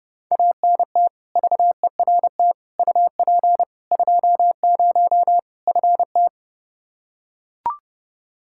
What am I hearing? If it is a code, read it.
Morse code, heard as a keyed tone, about 30 wpm, 700 Hz. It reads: ANT VERT UP 20 FT
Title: K1USN SST QSO Practice 30wpm